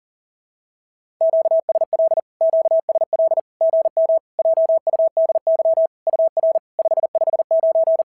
Morse code, heard as a keyed tone, about 40 wpm, 690 Hz. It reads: QSL QSL GM JUDY UR 559
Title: SOTA QSO Practice 40wpm